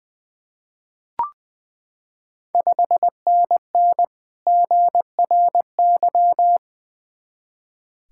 Morse code, heard as a keyed tone, about 20 wpm, 690 Hz. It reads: 5NN GRY
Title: Ontario QSO Party Exchange 20wpm